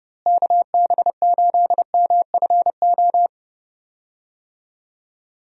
Morse code, heard as a keyed tone, about 30 wpm, 700 Hz. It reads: KB8MFO
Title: USA Call Signs - 6 Characters - Repeated 2x - 30wpm